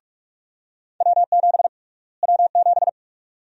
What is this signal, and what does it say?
Morse code, heard as a keyed tone, about 42 wpm, 690 Hz. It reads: W7 W7
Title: USA Call Sign Prefixes - Repeated 2x - 45wpm